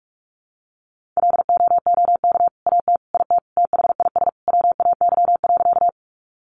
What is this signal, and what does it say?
Morse code, heard as a keyed tone, about 44 wpm, 700 Hz. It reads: LOOK AT IT THIS WAY.
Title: Sentences from Top 100 Words 45wpm